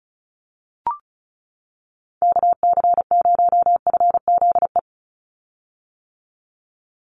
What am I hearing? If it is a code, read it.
Morse code, heard as a keyed tone, about 34 wpm, 700 Hz. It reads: KC0FZE
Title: USA Call Signs 35wpm